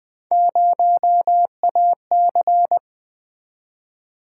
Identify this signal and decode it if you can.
Morse code, heard as a keyed tone, about 20 wpm, 700 Hz. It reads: 0AC
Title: Three Letter-Number Combinations - 20wpm